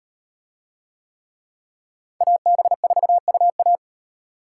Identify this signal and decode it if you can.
Morse code, heard as a keyed tone, about 36 wpm, 700 Hz. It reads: AB4UA